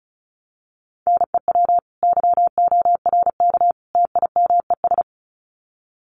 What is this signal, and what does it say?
Morse code, heard as a keyed tone, about 34 wpm, 700 Hz. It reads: NEW YORK TIMES
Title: Sets of 3 Words 35wpm